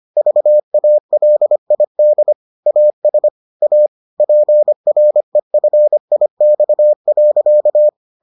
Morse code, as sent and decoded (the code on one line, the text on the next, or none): ...- .- .-.. .. -.. / .- ... / .- / .--. .-. ..-. .. -..- .-.-.-
VALID AS A PREFIX.